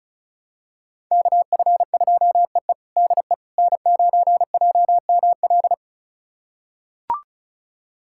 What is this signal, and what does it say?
Morse code, heard as a keyed tone, about 34 wpm, 700 Hz. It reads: KF2EE DE N9JML
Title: USA Callsign DE Callsign - 5 Characters - 35wpm